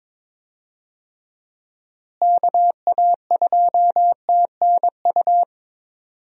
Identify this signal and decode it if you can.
Morse code, heard as a keyed tone, about 21 wpm, 700 Hz. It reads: KA2TNU
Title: USA Call Signs - 6 Characters - Repeated 2x - 22wpm